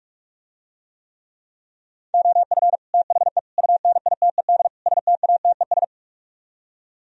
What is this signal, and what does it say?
Morse code, heard as a keyed tone, about 48 wpm, 700 Hz. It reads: OF THE UNITED STATES